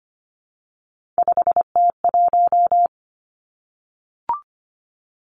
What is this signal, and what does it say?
Morse code, heard as a keyed tone, about 25 wpm, 700 Hz. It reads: 5T1
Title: Three Challenging Letter-Number Combinations 25wpm